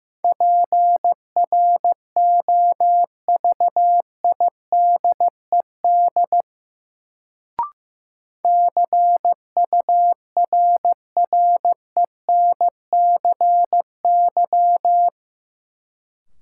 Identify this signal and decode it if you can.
Morse code, heard as a keyed tone, about 15 wpm, 700 Hz. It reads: PROVIDED CURRENCY
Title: Sentences from Top 1500 Words - New Word Drill Alt - 15wpm